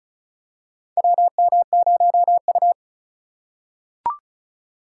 Morse code, heard as a keyed tone, about 36 wpm, 700 Hz. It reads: WM0U
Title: USA Call Signs - 4 Characters - 3x Character Spacing - 35wpm